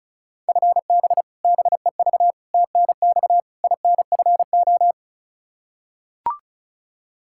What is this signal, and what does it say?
Morse code, heard as a keyed tone, about 36 wpm, 700 Hz. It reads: FB BEV TNX INFO
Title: Practice QSO - Exchange 3 - More Info - 35wpm